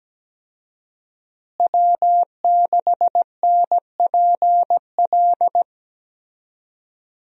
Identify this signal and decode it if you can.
Morse code, heard as a keyed tone, about 17 wpm, 700 Hz. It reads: W6NPL
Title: USA Call Signs - 5 Characters - Repeated 3x - 17wpm